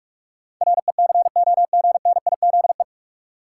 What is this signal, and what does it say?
Morse code, heard as a keyed tone, about 46 wpm, 700 Hz. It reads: RECOGNIZE
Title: Sentences from Top 1400 Words - New Word Drill - 45wpm